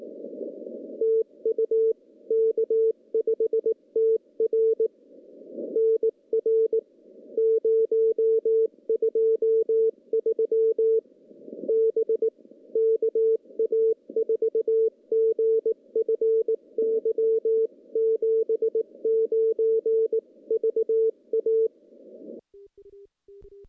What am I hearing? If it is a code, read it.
Morse code, heard as a keyed tone, about 19 wpm, 440 Hz. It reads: TU K5TR NR 023 B KA4GFY 79 VA
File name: contests / CW-SS-k5tr- -0200